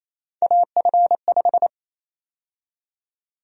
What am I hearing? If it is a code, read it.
Morse code, heard as a keyed tone, about 28 wpm, 700 Hz. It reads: AF5
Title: USA Call Sign Prefixes - Repeated 2x - 28wpm